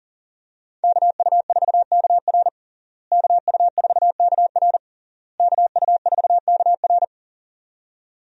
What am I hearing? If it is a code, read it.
Morse code, heard as a keyed tone, about 40 wpm, 700 Hz. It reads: KU4KR KU4KR KU4KR
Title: USA Call Signs - 5 Characters - Repeated 3x - 40wpm